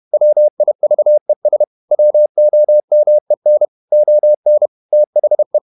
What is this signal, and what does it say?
Morse code, heard as a keyed tone, about 31 wpm, 600 Hz. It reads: WIVES WOMEN ON THE
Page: Never 31wpm | CW med Gnister